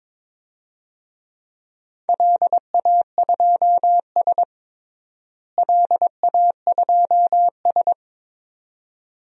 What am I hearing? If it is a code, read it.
Morse code, heard as a keyed tone, about 22 wpm, 700 Hz. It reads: LA2S LA2S
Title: DXCC-100 Call Signs - 4 Characters - Repeated 2x - 22wpm